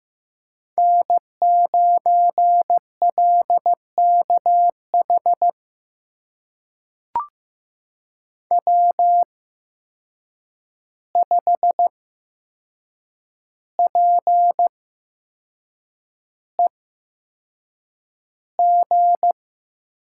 Morse code, heard as a keyed tone, about 15 wpm, spaced as for 6 wpm, 700 Hz. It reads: N9LKH W5PEG
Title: USA Call Signs - 5 Characters - 8x Character Spacing - 15wpm